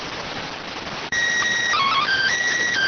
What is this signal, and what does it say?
SSTV (slow-scan television), PD 290.